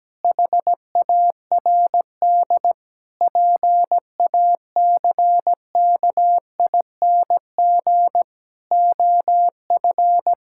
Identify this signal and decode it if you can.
Morse code, heard as a keyed tone, about 17 wpm, 700 Hz. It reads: HARD PACKING OF
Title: Pangram Challenge v2 - 17wpm